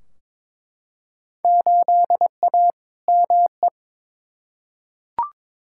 Morse code, heard as a keyed tone, about 22 wpm, 700 Hz. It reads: 8A ME
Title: Field Day Exchanges - 22wpm